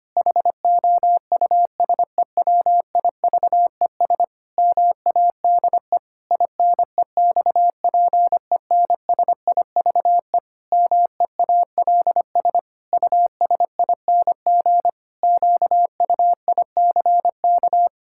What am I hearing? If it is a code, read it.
Morse code, heard as a keyed tone, about 25 wpm, 700 Hz. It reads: HOUSEWIVES MADE INEXPENSIVE MEALS USING QUICK
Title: Pangram Challenge v2 - 25wpm